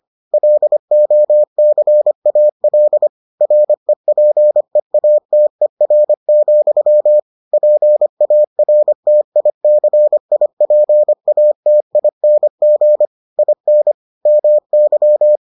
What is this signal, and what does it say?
Morse code, heard as a keyed tone, about 25 wpm, 600 Hz. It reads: LOCAL REPEATER, PARTICIPATING IN MY